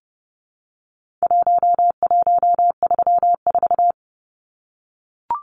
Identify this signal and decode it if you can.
Morse code, heard as a keyed tone, about 30 wpm, 700 Hz. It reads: 1134
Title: Numbers - Contest Style - 4-Digit 30wpm